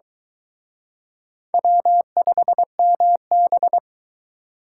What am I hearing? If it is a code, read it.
Morse code, heard as a keyed tone, about 24 wpm, 700 Hz. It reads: W5MB